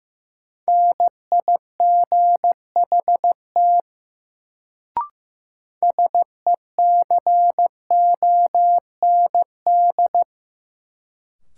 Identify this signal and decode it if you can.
Morse code, heard as a keyed tone, about 15 wpm, 700 Hz. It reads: NIGHT SECOND
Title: Sentences from Top 300 Words - New Word Drill - 15wpm